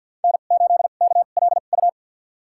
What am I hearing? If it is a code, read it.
Morse code, heard as a keyed tone, about 53 wpm, spaced as for 37 wpm, 700 Hz. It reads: N8KLU
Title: USA Call Signs - 5 Characters - 2x Character Spacing - 50wpm